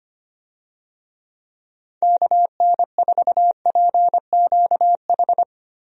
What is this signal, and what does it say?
Morse code, heard as a keyed tone, about 25 wpm, 700 Hz. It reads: KN4PQH
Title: USA Call Signs - 6 Characters - Repeated 2x - 25wpm